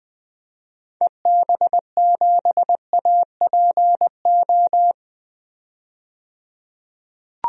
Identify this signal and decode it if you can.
Morse code, heard as a keyed tone, about 20 wpm, 700 Hz. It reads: EB7APO
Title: DXCC-100 Call Signs - 6 Characters - 20wpm